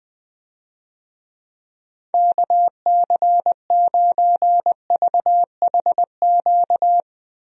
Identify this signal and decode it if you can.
Morse code, heard as a keyed tone, about 20 wpm, 700 Hz. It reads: KC9VHQ
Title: USA Call Signs - 6 Characters - Repeated 2x - 20wpm